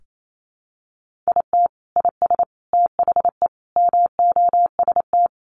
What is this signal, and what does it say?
Morse code, heard as a keyed tone, about 28 wpm, 700 Hz. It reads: IT IS THE MOST
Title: Sentences from Top 1500 Words 28wpm